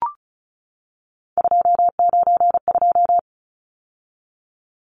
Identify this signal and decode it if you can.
Morse code, heard as a keyed tone, about 34 wpm, 700 Hz. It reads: 292
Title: Numbers - 3-Digit 35wpm